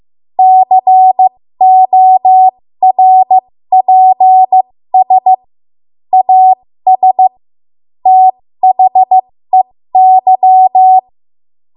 Morse code, heard as a keyed tone, about 15 wpm, spaced as for 13 wpm, 750 Hz. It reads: CORPS AS THEY
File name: Morse / 13WPM